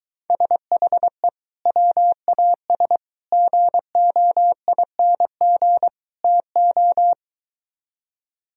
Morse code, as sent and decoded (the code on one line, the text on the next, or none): ... .... / .-- .- ... / --. --- .. -. --. / - ---
SHE WAS GOING TO